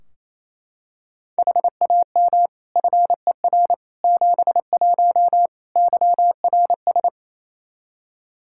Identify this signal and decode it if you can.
Morse code, heard as a keyed tone, about 28 wpm, 700 Hz. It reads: HAM FER 71 YRS
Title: K1USN SST QSO Practice 28wpm